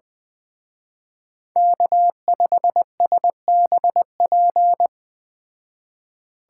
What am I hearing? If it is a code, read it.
Morse code, heard as a keyed tone, about 20 wpm, 700 Hz. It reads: K5SBP
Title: Speed Racing - USA Call Signs 17wpm